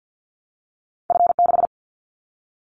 Morse code, heard as a keyed tone, about 50 wpm, 700 Hz. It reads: F6
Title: Callsign Prefixes for Top 25 DXCC - Repeated 3x - 50wpm